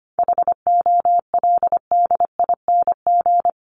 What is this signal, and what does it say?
Morse code, heard as a keyed tone, about 25 wpm, 700 Hz. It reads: HOLDING
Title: ING Suffix - Top 100 Words with Base Words 25wpm